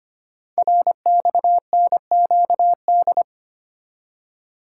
Morse code, heard as a keyed tone, about 25 wpm, 700 Hz. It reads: RXNQD
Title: Code Groups - 5 Letters - 25wpm